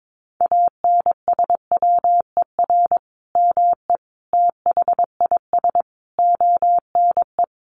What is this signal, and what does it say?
Morse code, heard as a keyed tone, about 22 wpm, 700 Hz. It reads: ANSWER ME THIS ONE
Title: Sentences from Top 400 Words 22wpm